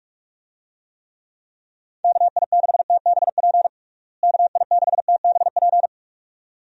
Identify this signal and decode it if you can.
Morse code, heard as a keyed tone, about 46 wpm, 700 Hz. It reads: KI6TBP KI6TBP
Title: USA Call Signs - 6 Characters - Repeated 2x - 45wpm